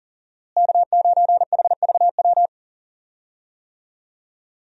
Morse code, heard as a keyed tone, about 40 wpm, 700 Hz. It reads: K9HVW
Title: USA Call Signs - 5 Characters - Repeated 2x - 40wpm